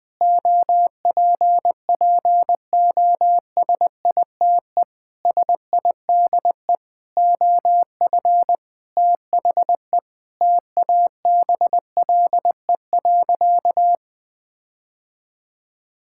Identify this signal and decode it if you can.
Morse code, heard as a keyed tone, about 19 wpm, 700 Hz. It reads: OPPOSITE SIDE OF THE TABLE.
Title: Sentences with 6-Word Phrases - Spoken Beginning - 20wpm